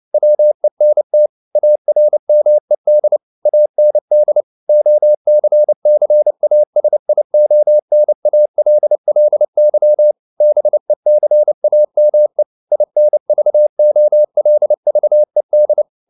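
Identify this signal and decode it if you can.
Morse code, heard as a keyed tone, about 29 wpm, 600 Hz. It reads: WENT ARMED AND OCCASIONALLY BECAME INVOLVED
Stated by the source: Morse code at 29 wpm, sending Ken Follett's novel Never in English, from CW med Gnister.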